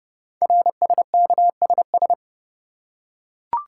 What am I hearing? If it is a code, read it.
Morse code, heard as a keyed tone, about 29 wpm, 700 Hz. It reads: RSKSS
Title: Code Groups - 5 Letters - 30wpm